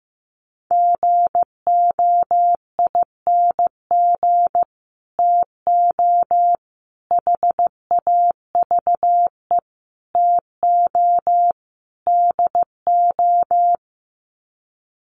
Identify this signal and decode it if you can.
Morse code, heard as a keyed tone, about 15 wpm, 700 Hz. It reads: GOING TO HAVE TO DO
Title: Sets of 5 Words Encore 15wpm